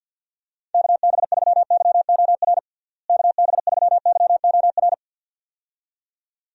Morse code, heard as a keyed tone, about 48 wpm, 700 Hz. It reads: KB3YOR KB3YOR
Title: USA Call Signs - 6 Characters - Repeated 2x - 50wpm